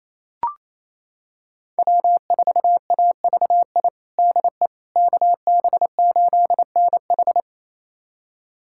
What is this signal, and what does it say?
Morse code, heard as a keyed tone, about 28 wpm, 700 Hz. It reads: W4AVI DE KB8NH
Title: USA Callsign DE Callsign - 5 Characters - 28wpm